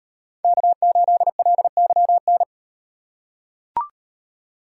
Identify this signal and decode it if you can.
Morse code, heard as a keyed tone, about 37 wpm, 700 Hz. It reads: K8LYN